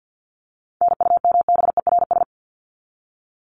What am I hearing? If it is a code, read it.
Morse code, heard as a keyed tone, about 50 wpm, 700 Hz. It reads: NUMBERS
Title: ES:S Suffix - Top 100 Words with Base Words 50wpm